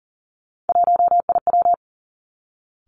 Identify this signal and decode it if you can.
Morse code, heard as a keyed tone, about 38 wpm, 690 Hz. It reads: 1IW
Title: Three Challenging Letter-Number Combinations 40wpm